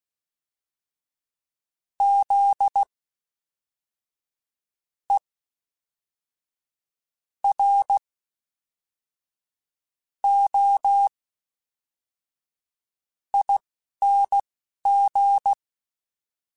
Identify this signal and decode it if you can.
Morse code, heard as a keyed tone, about 16 wpm, spaced as for 5 wpm, 800 Hz. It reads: ZEROING
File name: Special INGs / Special INGs-mp3s / Fast ing @10